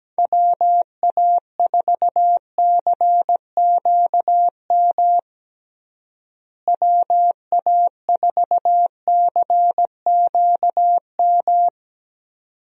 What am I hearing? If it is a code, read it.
Morse code, heard as a keyed tone, about 17 wpm, 700 Hz. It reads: WA4CQM WA4CQM
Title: USA Call Signs - 6 Characters - Repeated 2x - 17wpm